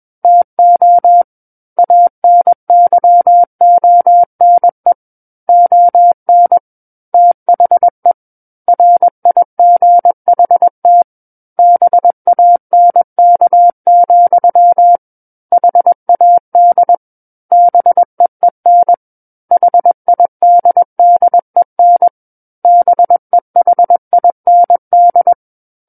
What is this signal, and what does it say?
Morse code, heard as a keyed tone, about 21 wpm, 700 Hz. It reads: TO ANYONE ON THE RIGHT BANK, HAD BEEN HIDDEN BEHIND
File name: Book One / 12-Chapter 12 - 21 WPM